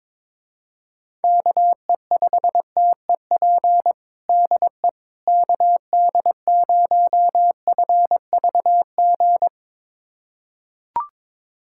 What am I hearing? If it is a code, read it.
Morse code, heard as a keyed tone, about 22 wpm, 700 Hz. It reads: KE5TEP DE KD0FVG
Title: USA Callsign DE Callsign - 6 Characters - 22wpm